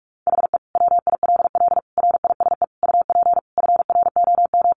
Morse code, heard as a keyed tone, about 44 wpm, 700 Hz. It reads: HE WILL RISE UP FROM
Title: Sentences from Top 900 Words 45wpm